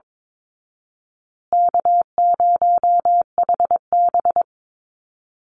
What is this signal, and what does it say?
Morse code, heard as a keyed tone, about 22 wpm, 700 Hz. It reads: K0HB
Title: Easy Call Signs Repeated 3x - 22wpm